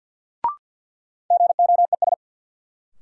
Morse code, heard as a keyed tone, about 52 wpm, 700 Hz. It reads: GOES
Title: Sentences from Top 1500 Words - New Word Drill Alt - 50wpm